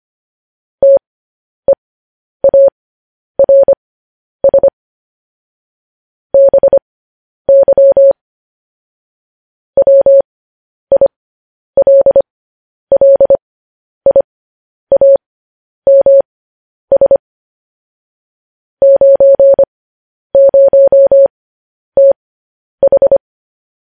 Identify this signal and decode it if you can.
Morse code, heard as a keyed tone, about 25 wpm, spaced as for 10 wpm, 550 Hz. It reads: TEARS BY WILLIAMS 90TH
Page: news-headlines-in-morse-code-at-10-wpm